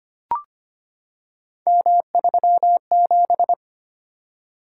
Morse code, heard as a keyed tone, about 25 wpm, 700 Hz. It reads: M37
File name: Three Letter-Number Combinations - 25wpm